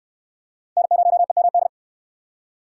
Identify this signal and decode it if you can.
Morse code, heard as a keyed tone, about 69 wpm, 700 Hz. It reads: N0EMD